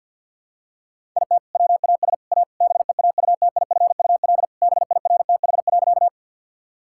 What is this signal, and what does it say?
Morse code, heard as a keyed tone, about 50 wpm, 700 Hz. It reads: IT WAS A BEAUTIFUL BIRTH.
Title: Sentences from Top 1000 Words - Review Part 5 - 50wpm